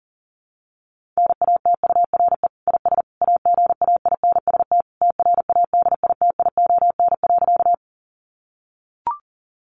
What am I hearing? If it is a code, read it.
Morse code, heard as a keyed tone, about 40 wpm, 700 Hz. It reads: NATURE IS AGAINST TRADITION.
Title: Sentences from Top 1500 Words 40wpm